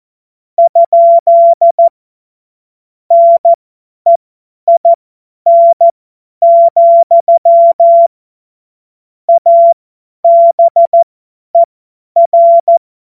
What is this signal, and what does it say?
Morse code, heard as a keyed tone, about 14 wpm, spaced as for 10 wpm, 680 Hz. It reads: ? NEIN, ABER
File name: Morsekurs / Witze / Witz 28.November 2023 50 F70